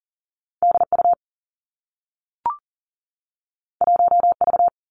Morse code, heard as a keyed tone, about 38 wpm, 700 Hz. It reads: D U 1 V
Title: Two Challenging Letter-Number Combinations 40wpm